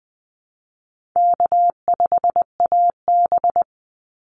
Morse code, heard as a keyed tone, about 20 wpm, 700 Hz. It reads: K5AB
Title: Easy Call Signs Repeated 2x - 20wpm